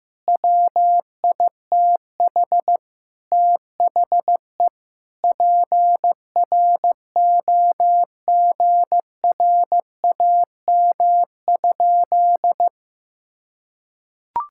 Morse code, heard as a keyed tone, about 16 wpm, 700 Hz. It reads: WITH THE PROGRAM?
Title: Sentences from Top 1000 Words - Review Part 5 - 15wpm